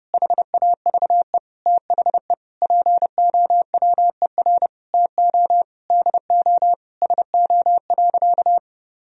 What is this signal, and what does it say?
Morse code, heard as a keyed tone, about 30 wpm, 700 Hz. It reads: HAVE THE POWER TO DO SO.
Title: Sentences from Top 1000 Words - Review Part 4 - 30wpm